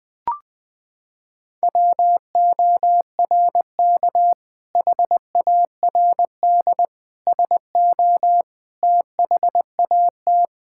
Morse code, heard as a keyed tone, about 20 wpm, 700 Hz. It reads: WORK HARD SO THAT
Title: Sentences with 2-Word Phrases - 20wpm